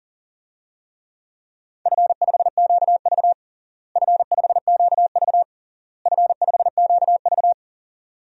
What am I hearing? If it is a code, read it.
Morse code, heard as a keyed tone, about 37 wpm, 700 Hz. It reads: F5QV F5QV F5QV
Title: DXCC-100 Call Signs - 4 Characters - Repeated 3x - 40wpm